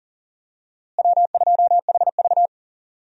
Morse code, heard as a keyed tone, about 40 wpm, 700 Hz. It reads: W2HV
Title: USA Call Signs - 4 Characters - Repeated 3x - 40wpm